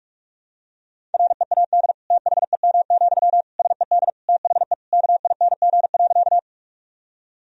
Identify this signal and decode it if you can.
Morse code, heard as a keyed tone, about 42 wpm, 700 Hz. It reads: READ THEM, SED THE KING.
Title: Sentences from Top 400 Words and Abbreviations 45wpm